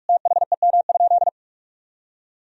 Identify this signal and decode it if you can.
Morse code, heard as a keyed tone, about 46 wpm, 700 Hz. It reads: THEM?
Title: Sentences from Top 1000 Words - Review Part 1 - 45wpm